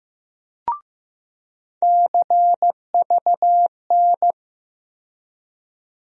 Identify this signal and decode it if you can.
Morse code, heard as a keyed tone, about 15 wpm, 700 Hz. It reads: CVN
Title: USA Call Sign Postfixes - 3 Letters 15wpm